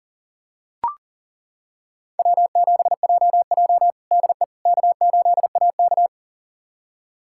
Morse code, heard as a keyed tone, about 37 wpm, 700 Hz. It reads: W7JJ DE K8AK
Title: USA Callsign DE Callsign - 4 Characters - 40wpm